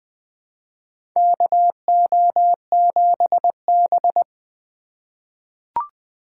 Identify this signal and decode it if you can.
Morse code, heard as a keyed tone, about 20 wpm, 700 Hz. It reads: KO7B
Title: USA Call Signs - 4 Characters - 2x Character Spacing - 20wpm